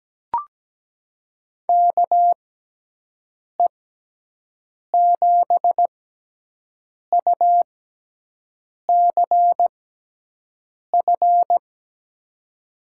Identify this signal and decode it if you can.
Morse code, heard as a keyed tone, about 17 wpm, spaced as for 6 wpm, 700 Hz. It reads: KE7UCF
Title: USA Call Signs - 6 Characters - 6x Character Spacing - 17wpm